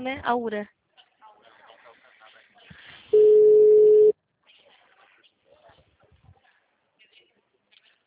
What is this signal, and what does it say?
Ringback tone (Europe, 425 Hz), with speech.